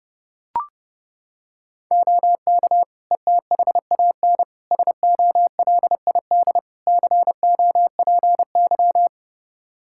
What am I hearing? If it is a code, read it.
Morse code, heard as a keyed tone, about 30 wpm, 700 Hz. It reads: OK ETHAN SOLID COPY
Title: Practice QSO - Exchange 3 - More Info - 30wpm